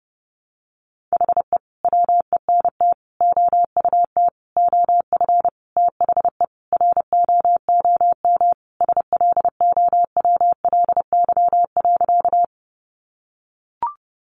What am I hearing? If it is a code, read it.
Morse code, heard as a keyed tone, about 30 wpm, 700 Hz. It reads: HE WENT OUT OF THE ROOM SLOWLY.
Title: Sentences from Top 1100 Words 30wpm